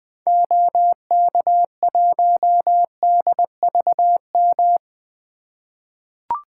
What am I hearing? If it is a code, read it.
Morse code, heard as a keyed tone, about 20 wpm, 700 Hz. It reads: OK1DVM
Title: 2023 SOTA Activator Call Signs - 20wpm